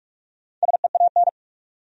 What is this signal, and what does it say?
Morse code, heard as a keyed tone, about 46 wpm, 700 Hz. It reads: SEAN
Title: USA Common Names - Max 4 Letters - 45wpm